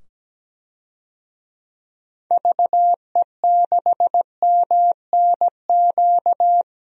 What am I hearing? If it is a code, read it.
Morse code, heard as a keyed tone, about 17 wpm, 700 Hz. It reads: VE6MNQ
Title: DXCC-100 Call Signs - 6 Characters - 17wpm